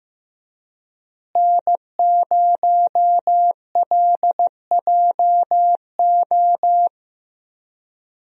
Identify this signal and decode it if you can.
Morse code, heard as a keyed tone, about 15 wpm, 700 Hz. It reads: N0LJO
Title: USA Call Signs - 5 Characters - Repeated 3x - 15wpm